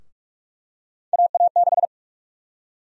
Morse code, heard as a keyed tone, about 42 wpm, 700 Hz. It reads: AA6
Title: USA Call Sign Prefixes - 45wpm